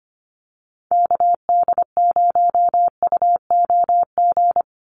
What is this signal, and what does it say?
Morse code, heard as a keyed tone, about 25 wpm, 700 Hz. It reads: KD0UOG
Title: USA Call Signs 25wpm